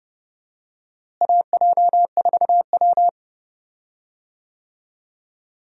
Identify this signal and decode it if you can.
Morse code, heard as a keyed tone, about 31 wpm, 700 Hz. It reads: AJ4W
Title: USA Call Signs - 4 Characters - Repeated 2x - 30wpm